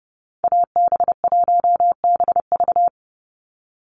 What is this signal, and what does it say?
Morse code, heard as a keyed tone, about 28 wpm, 700 Hz. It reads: AB1BV
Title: USA Call Signs 30wpm